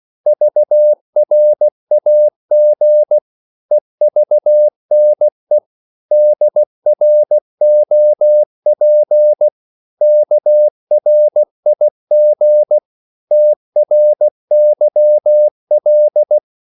Morse code, heard as a keyed tone, about 16 wpm, 600 Hz. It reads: VRAG EVNE DROP KRIG TRYL
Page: Korte danske ord 16wpm